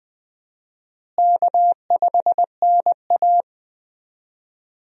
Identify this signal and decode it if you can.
Morse code, heard as a keyed tone, about 20 wpm, 700 Hz. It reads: K5NA
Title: USA Call Signs - 4 Characters - Repeated 3x - 20wpm